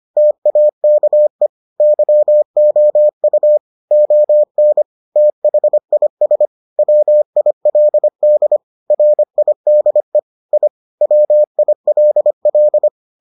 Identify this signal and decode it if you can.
Morse code, heard as a keyed tone, about 25 wpm, 600 Hz. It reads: TAKE YOU ON THIS WILD RIDE I WILL